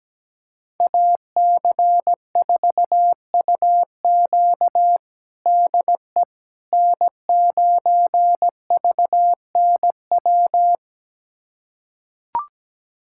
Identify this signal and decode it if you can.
Morse code, heard as a keyed tone, about 17 wpm, 700 Hz. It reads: AC4UQ DE N9VNW
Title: USA Callsign DE Callsign - 5 Characters - 17wpm